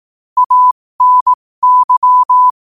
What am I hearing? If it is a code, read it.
Morse code, heard as a keyed tone, about 18 wpm, spaced as for 15 wpm, 1000 Hz. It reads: ANY
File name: qotdfiles / 15wpm